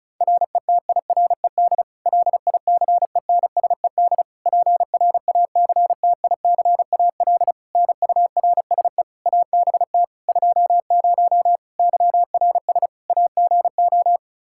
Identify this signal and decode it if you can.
Morse code, heard as a keyed tone, about 34 wpm, 700 Hz. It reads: RETIRED LICENSED PRACTICAL NURSE ABT 20 YRS AGO
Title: Practice QSO - Exchange 3 - More Info - 35wpm